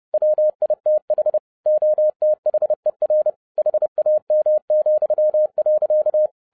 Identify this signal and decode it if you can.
Morse code, heard as a keyed tone, about 30 wpm, 610 Hz. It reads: WITH OTHER HAM,.